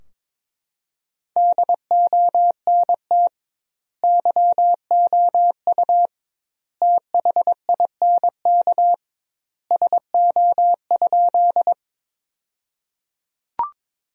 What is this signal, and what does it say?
Morse code, heard as a keyed tone, about 22 wpm, 700 Hz. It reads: DONT YOU THINK SO?
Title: 2x Word-Spacing - Sentences from Top 300 Words - 22wpm